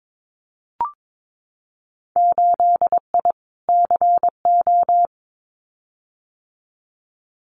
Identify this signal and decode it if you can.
Morse code, heard as a keyed tone, about 22 wpm, 700 Hz. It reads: 8I CO
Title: Winter Field Day Exchanges Encore 22wpm